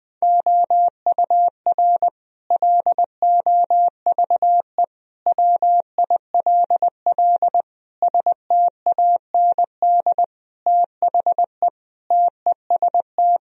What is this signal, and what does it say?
Morse code, heard as a keyed tone, about 20 wpm, 700 Hz. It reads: OUR LOVE WILL STAND THE TEST
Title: Sentences from Top 1000 Words - Review Part 3 - 20wpm